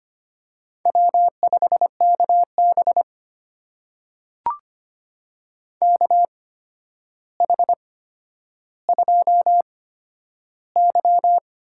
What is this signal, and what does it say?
Morse code, heard as a keyed tone, about 25 wpm, spaced as for 11 wpm, 700 Hz. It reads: W5KB KH2Y
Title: USA Call Signs - 4 Characters - 8x Character Spacing - 25wpm